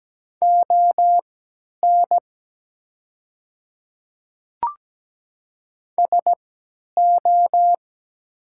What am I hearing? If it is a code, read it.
Morse code, heard as a keyed tone, about 17 wpm, spaced as for 10 wpm, 700 Hz. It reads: ON SO
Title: 3x Character Spacing - Top 300 Words - 17wpm